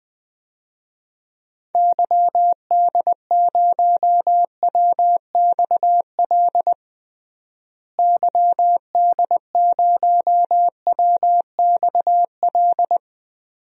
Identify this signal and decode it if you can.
Morse code, heard as a keyed tone, about 20 wpm, 700 Hz. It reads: YD0WXL YD0WXL
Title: DXCC-100 Call Signs - 6 Characters - Repeated 2x - 20wpm